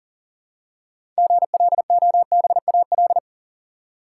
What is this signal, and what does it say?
Morse code, heard as a keyed tone, about 40 wpm, 700 Hz. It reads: GLOBAL
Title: Six Letter Words - 40wpm